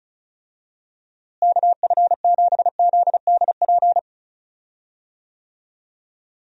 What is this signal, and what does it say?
Morse code, heard as a keyed tone, about 34 wpm, 700 Hz. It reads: KF7ZDP
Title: USA Call Signs - 6 Characters - Repeated 3x - 35wpm